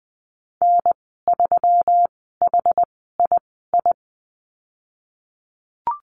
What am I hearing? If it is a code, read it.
Morse code, heard as a keyed tone, about 20 wpm, spaced as for 15 wpm, 700 Hz. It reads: N3HII
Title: 161 Lesson 26g - Letter g - Call Signs 20wpm